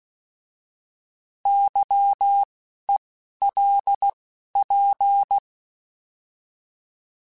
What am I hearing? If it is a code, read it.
Morse code, heard as a keyed tone, about 16 wpm, 800 Hz. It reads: Y E L P
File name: Misc / Misc-mp3s / Misc @10